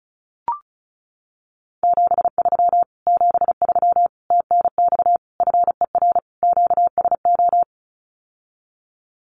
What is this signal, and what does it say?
Morse code, heard as a keyed tone, about 36 wpm, 700 Hz. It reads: 73 73 TNX FER QSO
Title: Common QSO Phrases Encore 35wpm